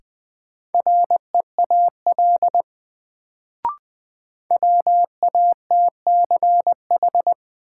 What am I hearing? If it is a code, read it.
Morse code, heard as a keyed tone, about 20 wpm, 700 Hz. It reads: REAL WATCH
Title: Sentences from Top 300 Words - New Word Drill - 20wpm